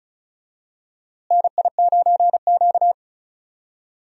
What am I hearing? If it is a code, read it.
Morse code, heard as a keyed tone, about 36 wpm, 700 Hz. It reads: NI9Q
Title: USA Call Signs - 4 Characters - Repeated 2x - 35wpm